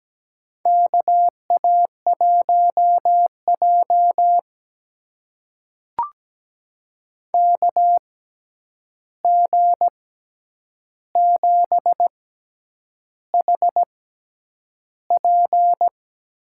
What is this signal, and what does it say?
Morse code, heard as a keyed tone, about 17 wpm, spaced as for 8 wpm, 700 Hz. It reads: KA1J KG7HP
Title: USA Call Signs - 6x Character Spacing - 17wpm